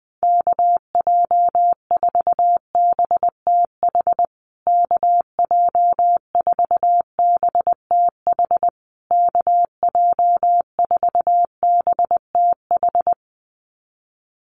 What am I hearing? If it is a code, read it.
Morse code, heard as a keyed tone, about 20 wpm, 700 Hz. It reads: KJ4BTH KJ4BTH KJ4BTH
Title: USA Call Signs Repeated 3x 20wpm